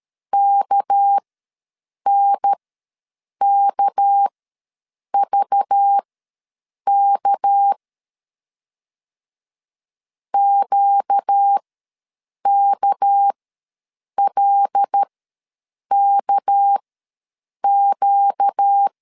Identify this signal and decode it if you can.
Morse code, heard as a keyed tone, about 13 wpm, spaced as for 7 wpm, 800 Hz. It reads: KNKVK QKLKQ
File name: LEZIONI / 6 PKX0 / 6 PK ws 12wpm 3116